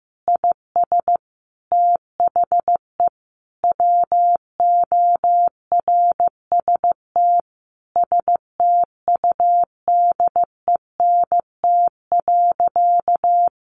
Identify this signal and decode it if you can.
Morse code, heard as a keyed tone, about 15 wpm, 700 Hz. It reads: IS THE WORST STUDENT.
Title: Additional Easy Phrases 15wpm